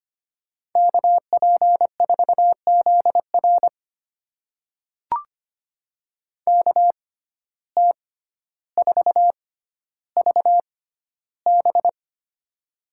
Morse code, heard as a keyed tone, about 25 wpm, spaced as for 13 wpm, 700 Hz. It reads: KP4ZR KT4VB
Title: USA Call Signs - 5 Characters - 6x Character Spacing - 25wpm